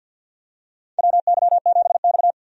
Morse code, heard as a keyed tone, about 51 wpm, 700 Hz. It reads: WY7X
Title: USA Call Signs - 4 Characters - 50wpm